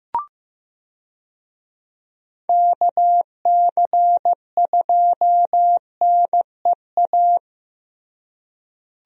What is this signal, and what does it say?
Morse code, heard as a keyed tone, about 15 wpm, 700 Hz. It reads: KC2NEA
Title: USA Call Signs - 6 Characters - Repeated 2x - 15wpm